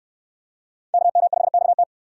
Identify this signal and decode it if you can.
Morse code, heard as a keyed tone, about 69 wpm, 700 Hz. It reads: KM5XT